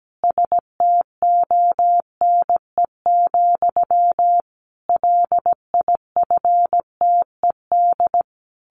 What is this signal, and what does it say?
Morse code, heard as a keyed tone, about 17 wpm, 700 Hz. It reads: STONE, LIFTED
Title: A Sound Of Thunder 17wpm